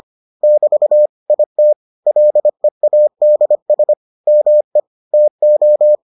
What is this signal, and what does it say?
Morse code, heard as a keyed tone, about 25 wpm, 600 Hz. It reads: <BT> IT LEADS ME TO